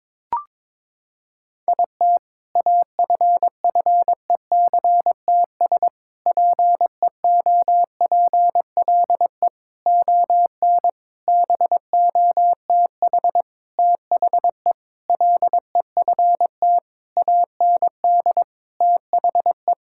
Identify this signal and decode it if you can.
Morse code, heard as a keyed tone, about 22 wpm, 700 Hz. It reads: IT AFFECTS PEOPLE ON BOTH THE LEFT AND THE
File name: Sentences with 7-Word Phrases - 22wpm